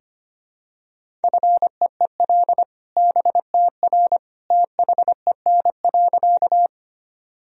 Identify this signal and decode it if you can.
Morse code, heard as a keyed tone, about 25 wpm, 700 Hz. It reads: FEEL BTR THEN.
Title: Sentences from Top 400 Words and Abbreviations 25wpm